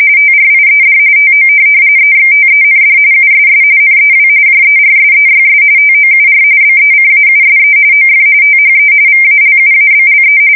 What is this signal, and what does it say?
RTTY, about 45.45 baud. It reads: TRIED TO EXTEND USS NEW JERSEYS TOUR BECAUSE OF THE VALUABLE
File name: fsk170 / USS-New-Jersey